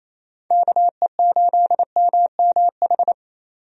Morse code, heard as a keyed tone, about 28 wpm, 700 Hz. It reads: KE8MMH
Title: USA Call Signs - 6 Characters - 28wpm